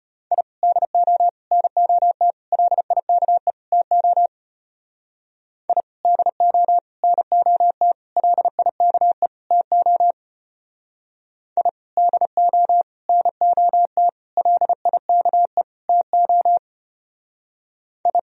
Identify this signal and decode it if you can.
Morse code, heard as a keyed tone, about 31 wpm, 700 Hz. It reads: I DO NOT LIKE TO I DO NOT LIKE TO I DO NOT LIKE TO I
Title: Speed-Racing - Sets of 5 Words 25wpm